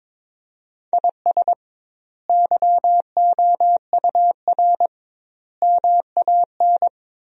Morse code, heard as a keyed tone, about 22 wpm, 700 Hz. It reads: IS YOUR MAN
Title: 2x Word-Spacing - Sentences from Top 400 Words - 22wpm